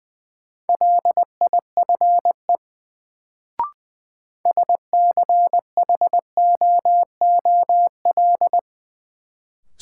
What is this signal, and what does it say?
Morse code, heard as a keyed tone, about 21 wpm, 700 Hz. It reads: LIFE SCHOOL
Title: Sentences from Top 300 Words - New Word Drill - 20wpm